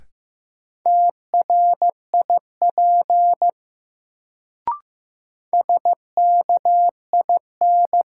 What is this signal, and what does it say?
Morse code, heard as a keyed tone, about 15 wpm, 700 Hz. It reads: TRIP SKIN
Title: Sentences from Top 1300 Words - New Word Drill Alt - 15wpm